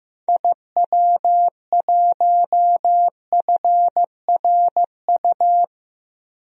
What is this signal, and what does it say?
Morse code, heard as a keyed tone, about 15 wpm, 700 Hz. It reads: IW1FRU
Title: DXCC-100 Call Signs - 6 Characters - 15wpm